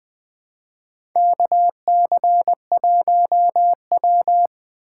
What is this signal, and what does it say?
Morse code, heard as a keyed tone, about 21 wpm, 700 Hz. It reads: KC1W
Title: USA Call Signs - 4 Characters - 2x Character Spacing - 20wpm